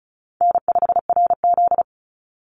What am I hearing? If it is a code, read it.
Morse code, heard as a keyed tone, about 34 wpm, 700 Hz. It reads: N5RZ
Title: Call Signs Repeated 2x - 35wpm